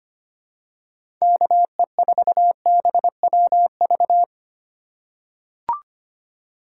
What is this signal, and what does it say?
Morse code, heard as a keyed tone, about 25 wpm, 700 Hz. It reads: KE4BWV
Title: USA Call Signs - 4x Character Spacing - 25wpm